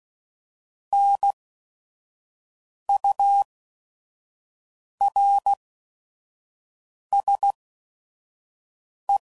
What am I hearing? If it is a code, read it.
Morse code, heard as a keyed tone, about 16 wpm, spaced as for 5 wpm, 800 Hz. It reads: NURSE
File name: Prefixes / By First Letter / Ns